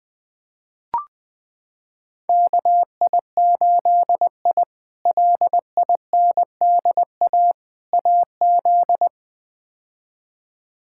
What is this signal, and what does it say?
Morse code, heard as a keyed tone, about 20 wpm, 700 Hz. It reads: KI8I LINDA AZ
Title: K1USN SST QSO Practice - 20wpm